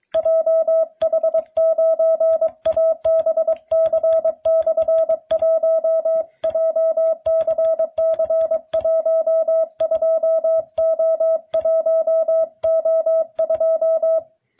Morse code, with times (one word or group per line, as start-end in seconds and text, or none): 0.2-6.2 JH9ABC/1
6.5-14.2 JCC12O1O2